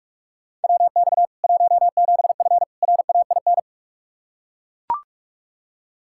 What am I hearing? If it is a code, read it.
Morse code, heard as a keyed tone, about 41 wpm, 700 Hz. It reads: WX 17F RAIN
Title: K1USN SST QSO Practice 45wpm